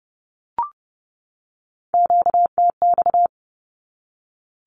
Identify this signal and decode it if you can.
Morse code, heard as a keyed tone, about 30 wpm, 700 Hz. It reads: QTX
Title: Q-Codes 30wpm